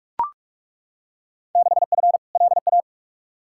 Morse code, heard as a keyed tone, about 41 wpm, 700 Hz. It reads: 6F LA
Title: Field Day Exchanges Encore - 45wpm